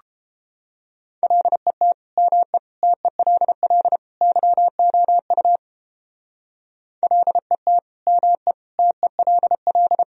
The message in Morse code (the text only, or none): LET ME TELL YOU LET ME TELL